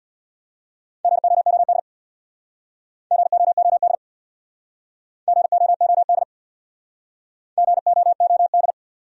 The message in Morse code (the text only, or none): GOOD GOOD GOOD GOOD